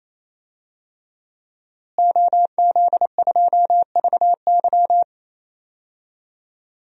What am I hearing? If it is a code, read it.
Morse code, heard as a keyed tone, about 28 wpm, 700 Hz. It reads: OZ2VY
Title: DXCC-100 Call Signs - 5 Characters - Repeated 3x - 28wpm